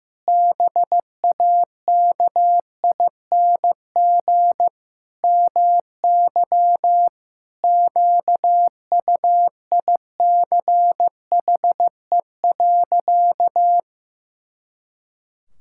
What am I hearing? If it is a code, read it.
Morse code, heard as a keyed tone, about 15 wpm, 700 Hz. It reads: BAKING MY QUICHE.
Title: Pangram Challenge v2 - 15wpm